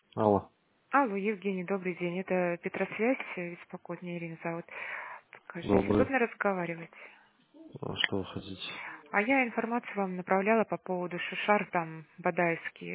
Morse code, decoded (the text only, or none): none